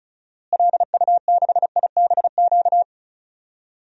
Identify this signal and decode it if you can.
Morse code, heard as a keyed tone, about 34 wpm, 700 Hz. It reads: LU6IBQ
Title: DXCC-100 Call Signs - 6 Characters - Repeated 2x - 35wpm